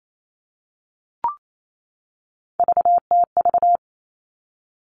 Morse code, heard as a keyed tone, about 28 wpm, 700 Hz. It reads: VTV
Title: All Cut Numbers - 3-Digit 28wpm